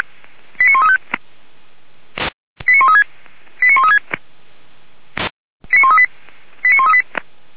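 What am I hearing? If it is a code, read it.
Five-tone selective call (ZVEI), heard as quick runs of five tones, tones of about 70 ms. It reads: 89135 89136 89136 89138 89138
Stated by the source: ZVEI five-tone selective call 89134 received via SDR.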